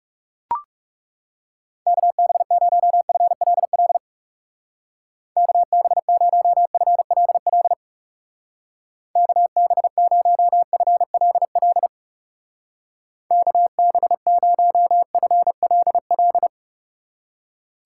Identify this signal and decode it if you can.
Morse code, heard as a keyed tone, about 38 wpm, 700 Hz. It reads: KB0FLL KB0FLL KB0FLL KB0FLL
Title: Speed Racing - USA 2x3 Call Signs 30wpm